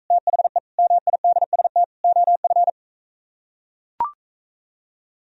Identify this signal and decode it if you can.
Morse code, heard as a keyed tone, about 44 wpm, 700 Hz. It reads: THE MIDST OF